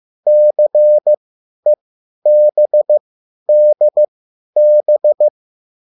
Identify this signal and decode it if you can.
Morse code, heard as a keyed tone, about 15 wpm, spaced as for 10 wpm, 600 Hz. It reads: CEBDB
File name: cwbook